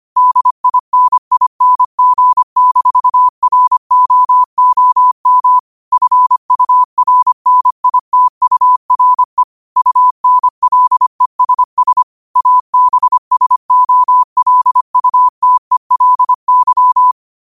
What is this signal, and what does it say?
Morse code, heard as a keyed tone, about 25 wpm, 1000 Hz. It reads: DINING-ROOM FURNITURE UNLESS ABSOLUTELY
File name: qotdfiles / 25wpm